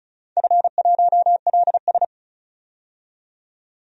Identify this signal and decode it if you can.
Morse code, heard as a keyed tone, about 36 wpm, 690 Hz. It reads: F1LS
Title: DXCC-100 Call Signs - 4 Characters - 35wpm